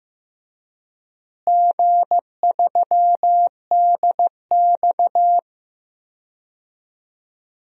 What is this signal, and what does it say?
Morse code, heard as a keyed tone, about 16 wpm, 700 Hz. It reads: G3DX
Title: DXCC-100 Call Signs - 4 Characters - Repeated 3x - 15wpm